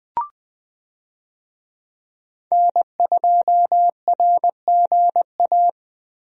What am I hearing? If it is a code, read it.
Morse code, heard as a keyed tone, about 20 wpm, 700 Hz. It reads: N2RGA
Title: USA Call Signs - 5 Characters - 20wpm